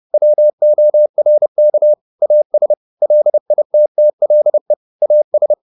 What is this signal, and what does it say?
Morse code, heard as a keyed tone, about 30 wpm, 600 Hz. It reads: WORK AS LITTLE AS
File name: the old man and the sea / the old man and the sea 1 30wpm